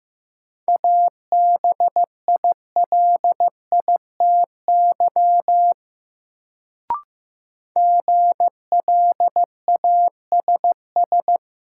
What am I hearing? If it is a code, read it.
Morse code, heard as a keyed tone, about 15 wpm, 700 Hz. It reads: ABILITY GLASS
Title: Sentences from Top 1000 Words - Review Part 3 - New Word Drill - 15wpm